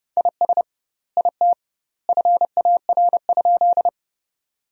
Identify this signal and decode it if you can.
Morse code, heard as a keyed tone, about 30 wpm, 700 Hz. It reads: IS IT FAR?
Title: 2x Word-Spacing - Sentences from Top 300 Words - 30wpm